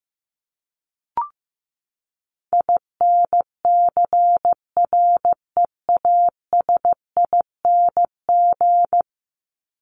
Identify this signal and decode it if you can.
Morse code, heard as a keyed tone, about 15 wpm, 700 Hz. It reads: INCREASING
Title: ING Suffix - Top 100 Words with Base Words 15wpm